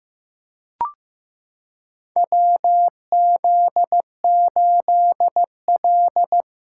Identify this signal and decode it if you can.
Morse code, heard as a keyed tone, about 15 wpm, 700 Hz. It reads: WZ8L
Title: USA Callsign DE Callsign - 4 Characters - 15wpm